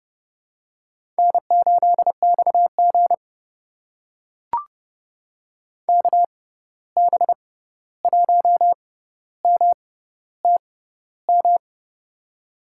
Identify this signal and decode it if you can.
Morse code, heard as a keyed tone, about 30 wpm, spaced as for 14 wpm, 700 Hz. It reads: N8XG KB1MTM
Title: USA Call Signs - 6x Character Spacing - 30wpm